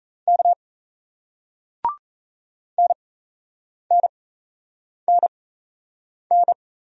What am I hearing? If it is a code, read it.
Morse code, heard as a keyed tone, about 38 wpm, spaced as for 8 wpm, 700 Hz. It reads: K NNNN